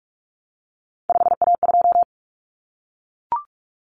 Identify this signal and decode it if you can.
Morse code, heard as a keyed tone, about 44 wpm, 700 Hz. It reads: 5A2
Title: Common Cut Numbers - 3-Digit 45wpm